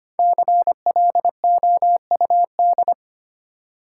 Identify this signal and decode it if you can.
Morse code, heard as a keyed tone, about 25 wpm, 700 Hz. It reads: CLOUD
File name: 5 Letter Words - First 3 Letters Given - 25wpm